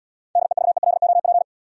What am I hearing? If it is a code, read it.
Morse code, heard as a keyed tone, about 66 wpm, 700 Hz. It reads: DEVELOP